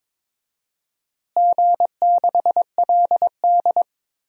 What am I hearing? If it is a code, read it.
Morse code, heard as a keyed tone, about 22 wpm, 700 Hz. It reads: G6LD
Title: DXCC-100 Call Signs - 4 Characters - Repeated 3x - 22wpm